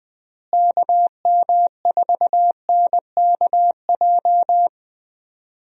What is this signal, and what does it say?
Morse code, heard as a keyed tone, about 20 wpm, 700 Hz. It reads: KM4NKJ
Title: USA Call Signs - 6 Characters - Repeated 2x - 20wpm